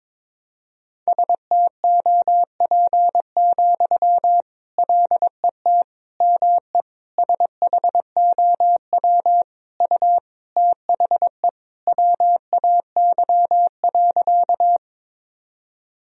Morse code, heard as a keyed tone, about 21 wpm, 700 Hz. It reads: STOP, LET ME SHOW U THE WAY.
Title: Sentences from Top 300 Words and Abbreviations 22wpm